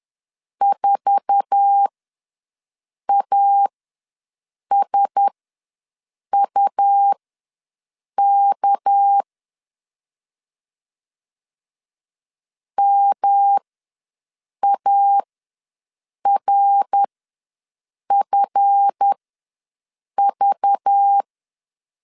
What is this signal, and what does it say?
Morse code, heard as a keyed tone, about 10 wpm, spaced as for 6 wpm, 800 Hz. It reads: 4ASUK MARFV
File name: LEZIONI / 8 YJ45 / 8 YJ4 ws 10wpm 4020